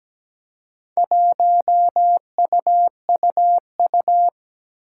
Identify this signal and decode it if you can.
Morse code, heard as a keyed tone, about 17 wpm, 700 Hz. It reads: 1UUU
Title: Four Challenging Letter-Number Combinations 17wpm